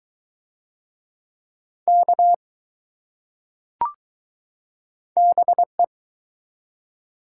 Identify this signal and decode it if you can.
Morse code, heard as a keyed tone, about 22 wpm, 700 Hz. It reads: K BE